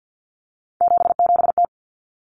Morse code, heard as a keyed tone, about 50 wpm, 700 Hz. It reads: 77T
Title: Common Cut Numbers - 3-Digit 50wpm